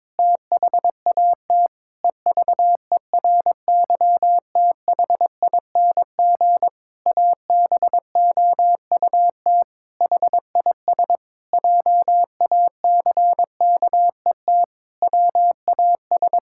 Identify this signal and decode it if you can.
Morse code, heard as a keyed tone, about 22 wpm, 700 Hz. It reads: THAT EVERYTHING ABOUT HIS JACKET WAS
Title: Pangram Challenge v2 - 22wpm